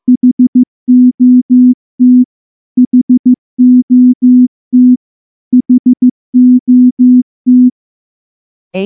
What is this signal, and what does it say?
Morse code, heard as a keyed tone, about 14 wpm, 250 Hz. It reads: HOT HOT HOT
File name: morse / morse-training-15WPM